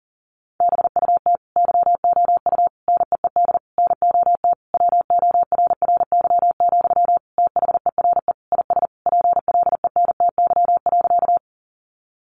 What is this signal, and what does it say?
Morse code, heard as a keyed tone, about 40 wpm, 700 Hz. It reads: BUT YOU NEED NOT WORRY, THERE IS PLENTY.
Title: Sentences from Top 1300 Words 40wpm